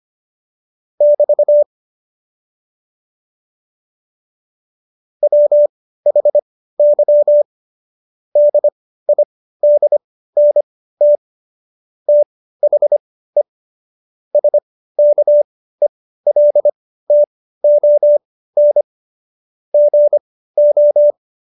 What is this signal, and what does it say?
Morse code, heard as a keyed tone, about 25 wpm, spaced as for 15 wpm, 590 Hz. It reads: <BT> WHY DIDNT THE SKELTON GO
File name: Jokes-25-char-15-eff